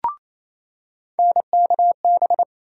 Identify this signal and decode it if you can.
Morse code, heard as a keyed tone, about 28 wpm, 700 Hz. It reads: NKB
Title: USA Call Sign Postfixes - 3 Letters 28wpm